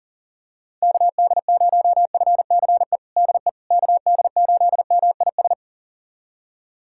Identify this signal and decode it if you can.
Morse code, heard as a keyed tone, about 40 wpm, 700 Hz. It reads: KD0FCE DE KD8MIS
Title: USA Callsign DE Callsign 40wpm